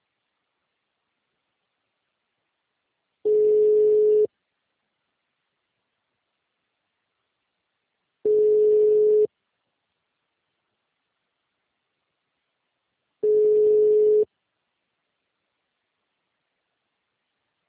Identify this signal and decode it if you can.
Ringback tone (Europe, 425 Hz).
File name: out- -2015